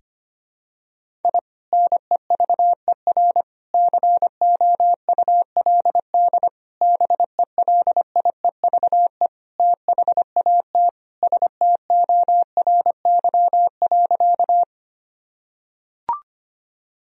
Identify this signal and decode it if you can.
Morse code, heard as a keyed tone, about 25 wpm, 700 Hz. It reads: I NEVER COULD BELIEVE THAT STORY.
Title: Sentences from Top 1000 Words - Review Part 1 - 25wpm